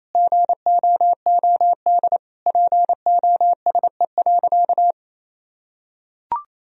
Morse code, heard as a keyed tone, about 28 wpm, 700 Hz. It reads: GOOD POSE.
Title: Sentences from Top 1000 Words - Review Part 2 - 28wpm